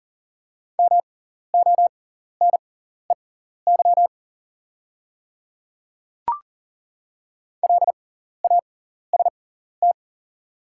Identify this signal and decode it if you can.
Morse code, heard as a keyed tone, about 40 wpm, spaced as for 14 wpm, 690 Hz. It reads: MONEY LAST
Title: 6x Character Spacing - Top 500 Words - 40wpm